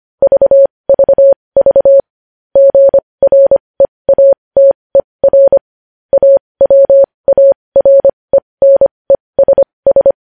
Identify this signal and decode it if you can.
Morse code, heard as a keyed tone, about 25 wpm, spaced as for 20 wpm, 550 Hz. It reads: VVV GREATER AWARENESS
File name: podcast / News headlines-1x-25x20